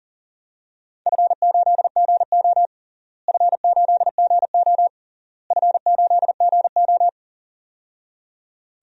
Morse code, heard as a keyed tone, about 40 wpm, 700 Hz. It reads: F8GO F8GO F8GO
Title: DXCC-100 Call Signs - 4 Characters - Repeated 3x - 40wpm